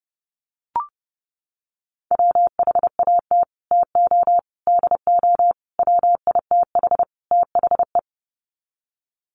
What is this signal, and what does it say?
Morse code, heard as a keyed tone, about 30 wpm, 700 Hz. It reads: WHAT TO DO WITH THE
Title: Sets of 5 Words Encore2 30wpm